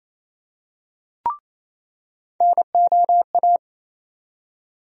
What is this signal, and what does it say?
Morse code, heard as a keyed tone, about 28 wpm, 700 Hz. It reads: NOA
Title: USA Call Sign Postfixes - 3 Letters 28wpm